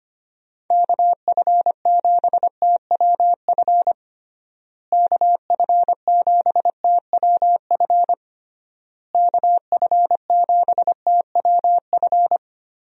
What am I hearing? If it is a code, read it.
Morse code, heard as a keyed tone, about 25 wpm, 700 Hz. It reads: KF7TWF KF7TWF KF7TWF
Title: USA Call Signs - 6 Characters - Repeated 3x - 25wpm